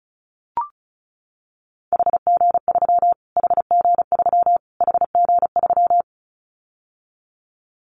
Morse code, heard as a keyed tone, about 34 wpm, 700 Hz. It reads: HG3 HG3 HG3
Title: Callsign Prefixes for Top 25 DXCC - Repeated 3x - 35wpm